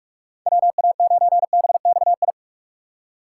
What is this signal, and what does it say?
Morse code, heard as a keyed tone, about 46 wpm, 700 Hz. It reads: WA9BXI
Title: USA Call Signs - 6 Characters - 4x Character Spacing - 45wpm